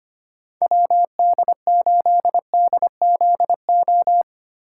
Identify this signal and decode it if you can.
Morse code, heard as a keyed tone, about 25 wpm, 700 Hz. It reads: WD8DZO
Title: USA Call Signs - 6 Characters - 2x Character Spacing - 25wpm